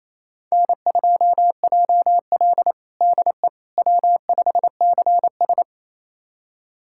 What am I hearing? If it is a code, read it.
Morse code, heard as a keyed tone, about 28 wpm, 700 Hz. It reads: N2JL DE W5CS
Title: USA Callsign DE Callsign - 4 Characters - 28wpm